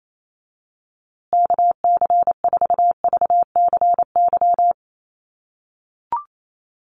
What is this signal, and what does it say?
Morse code, heard as a keyed tone, about 28 wpm, 700 Hz. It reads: KC4VCY
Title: USA Call Signs Repeated 2x 28wpm